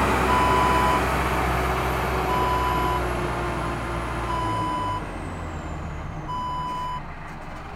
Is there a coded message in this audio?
No.